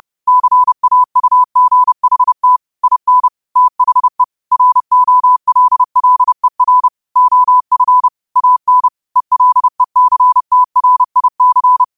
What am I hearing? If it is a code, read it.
Morse code, heard as a keyed tone, about 30 wpm, 1000 Hz. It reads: CAUGHT IN THE ROLLER OF AN ELECTRIC
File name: qotdfiles / 30wpm